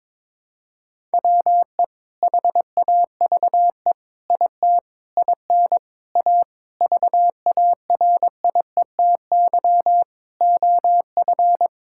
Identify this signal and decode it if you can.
Morse code, heard as a keyed tone, about 21 wpm, 700 Hz. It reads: WE HAVE IT IN A VARIETY OF
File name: practice / 22wpm / Sentences with 4-Word Phrases - 22wpm